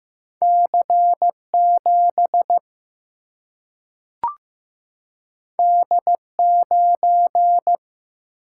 Morse code, heard as a keyed tone, about 15 wpm, 700 Hz. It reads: C 7 D 9
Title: Single-Letter Single-Number 15wpm